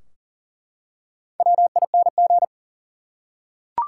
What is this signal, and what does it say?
Morse code, heard as a keyed tone, about 40 wpm, 700 Hz. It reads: WING
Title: Four Letter Words - 40wpm